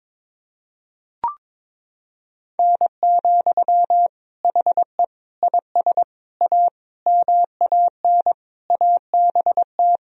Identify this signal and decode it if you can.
Morse code, heard as a keyed tone, about 22 wpm, 700 Hz. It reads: N, HE IS A MAN ABT
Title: Sentences from Top 400 Words and Abbreviations 22wpm